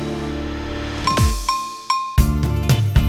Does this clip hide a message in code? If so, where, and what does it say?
No.